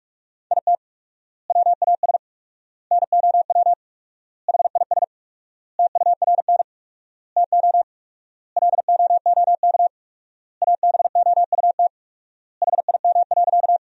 Morse code, heard as a keyed tone, about 46 wpm, 700 Hz. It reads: IT WAS NOW HIS TURN TO LOOK ABOUT HIM.
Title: 4x Word-Spacing - Sentences from Top 200 Words - 45wpm